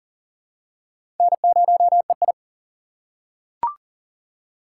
Morse code, heard as a keyed tone, about 39 wpm, 700 Hz. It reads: N0EI
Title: USA Call Signs - 4 Characters - 3x Character Spacing - 40wpm